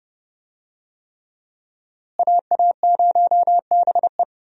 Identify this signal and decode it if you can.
Morse code, heard as a keyed tone, about 29 wpm, 700 Hz. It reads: AA0BE
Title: USA Call Signs - 5 Characters - Repeated 3x - 30wpm